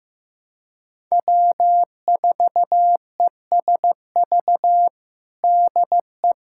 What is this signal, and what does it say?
Morse code, heard as a keyed tone, about 15 wpm, 700 Hz. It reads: W4ESV DE
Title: USA Callsign DE Callsign - 5 Characters - 15wpm